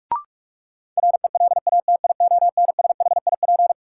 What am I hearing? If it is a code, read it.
Morse code, heard as a keyed tone, about 46 wpm, 700 Hz. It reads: RELATIONSHIP
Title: Sentences from Top 1100 Words - New Word Drill - 45wpm